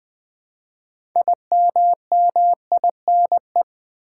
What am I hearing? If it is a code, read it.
Morse code, heard as a keyed tone, about 20 wpm, 700 Hz. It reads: IMMINE
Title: Build A Word - 2 Letters At A Time - 8-Letter Words - 20wpm